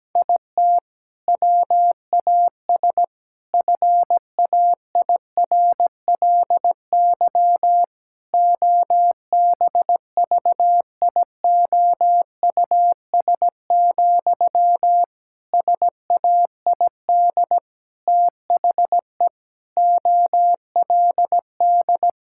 Morse code, heard as a keyed tone, about 17 wpm, 700 Hz. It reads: IT WAS FAIRLY OBVIOUS, SAID THE OLD
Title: Sentences from Top 1600 Words 17wpm